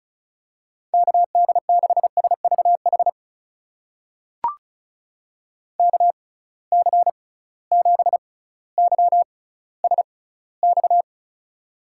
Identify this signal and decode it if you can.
Morse code, heard as a keyed tone, about 36 wpm, spaced as for 18 wpm, 700 Hz. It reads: KD6SVH KC7YSX
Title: USA Call Signs - 6 Characters - 6x Character Spacing - 35wpm